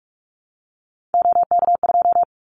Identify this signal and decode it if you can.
Morse code, heard as a keyed tone, about 44 wpm, 700 Hz. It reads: OK2
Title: Callsign Prefixes for Top 25 DXCC - Repeated 3x - 45wpm